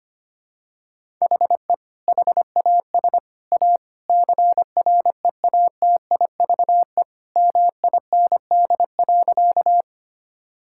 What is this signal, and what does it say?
Morse code, heard as a keyed tone, about 25 wpm, 700 Hz. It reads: HE HAS A CREATIVE MIND.
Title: Sentences from Top 1000 Words - Review Part 5 - 25wpm